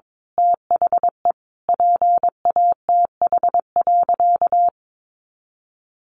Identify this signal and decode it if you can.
Morse code, heard as a keyed tone, about 22 wpm, 700 Hz. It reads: THE PATH.
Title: A Sound Of Thunder 22wpm